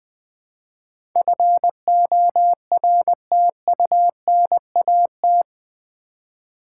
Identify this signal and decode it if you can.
Morse code, heard as a keyed tone, about 20 wpm, 700 Hz. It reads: FORTUNAT
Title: Build A Word - 4 Letters At A Time - 11-Letter Words - 20wpm